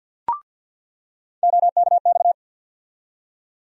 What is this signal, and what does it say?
Morse code, heard as a keyed tone, about 49 wpm, 700 Hz. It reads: OKX
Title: USA Call Sign Postfixes - 3 Letters 50wpm